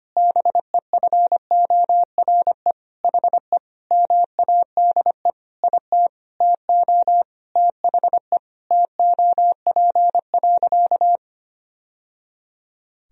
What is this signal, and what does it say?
Morse code, heard as a keyed tone, about 25 wpm, 700 Hz. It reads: BEFORE HE MADE IT TO THE TOP.
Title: Sentences from Top 1000 Words - Review Part 2 - 25wpm